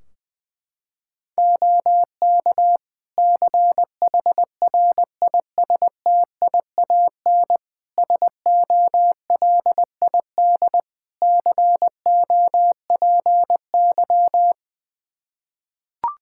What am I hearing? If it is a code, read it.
Morse code, heard as a keyed tone, about 20 wpm, 700 Hz. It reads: OK CHRISTIAN SOLID COPY
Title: Practice QSO - Exchange 3 - More Info - 20wpm